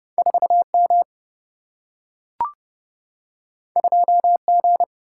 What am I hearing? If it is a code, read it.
Morse code, heard as a keyed tone, about 30 wpm, 700 Hz. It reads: 4M 2G
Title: Single-Letter Single-Number 30wpm